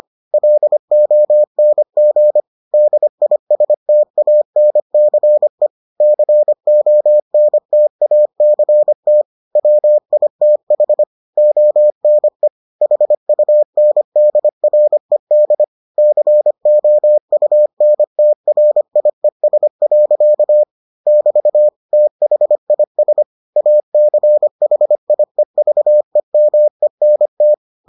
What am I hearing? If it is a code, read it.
Morse code, heard as a keyed tone, about 25 wpm, 600 Hz. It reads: LONG DISTANCE CONTACT WITH ONE HUNDRED COUNTRIES. <BT> THIS ACHIEVEMENT